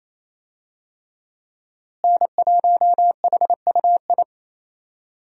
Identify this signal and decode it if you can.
Morse code, heard as a keyed tone, about 27 wpm, 700 Hz. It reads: N1HUI
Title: USA Call Signs - 5 Characters - Repeated 3x - 28wpm